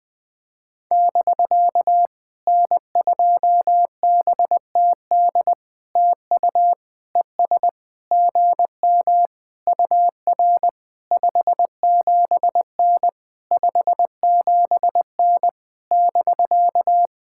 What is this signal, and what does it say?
Morse code, heard as a keyed tone, about 20 wpm, 700 Hz. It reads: <BK> N2BTD TU ES GM UR 57N 57N <BK>
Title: POTA QSO Practice - 20wpm